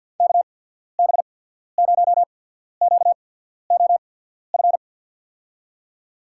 Morse code, heard as a keyed tone, about 50 wpm, spaced as for 14 wpm, 700 Hz. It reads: KB0QOF
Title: USA Call Signs - 6 Characters - 8x Character Spacing - 50wpm